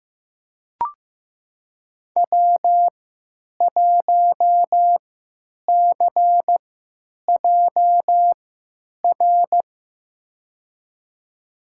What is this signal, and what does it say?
Morse code, heard as a keyed tone, about 15 wpm, spaced as for 9 wpm, 700 Hz. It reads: W1CJR
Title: USA Call Signs - 5 Characters - 3x Character Spacing - 15wpm